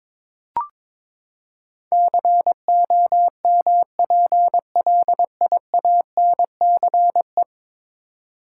Morse code, heard as a keyed tone, about 22 wpm, 700 Hz. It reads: COMPLIANCE
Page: morse-code-ninja-practice-22wpm